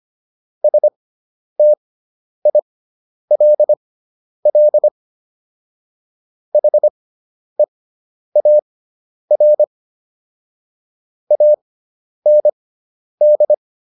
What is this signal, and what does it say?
Morse code, heard as a keyed tone, about 25 wpm, spaced as for 10 wpm, 600 Hz. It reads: STILL HEAR AND